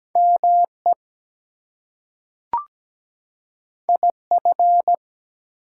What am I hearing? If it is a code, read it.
Morse code, heard as a keyed tone, about 17 wpm, 700 Hz. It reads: ME IF
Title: Two Letter Words - 17wpm